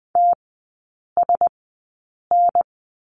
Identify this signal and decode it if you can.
Morse code, heard as a keyed tone, about 20 wpm, spaced as for 8 wpm, 700 Hz. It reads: TSN
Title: 035 Lesson 7e - Letter s - Three Characters 20wpm